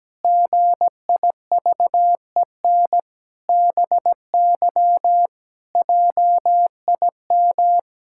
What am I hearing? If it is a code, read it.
Morse code, heard as a keyed tone, about 17 wpm, 700 Hz. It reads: GIVEN BY JIM
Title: Pangram Challenge v2 - 17wpm